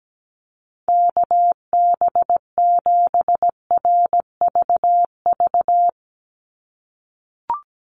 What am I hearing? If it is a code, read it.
Morse code, heard as a keyed tone, about 17 wpm, 700 Hz. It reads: KB7RVV
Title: USA Call Signs 17wpm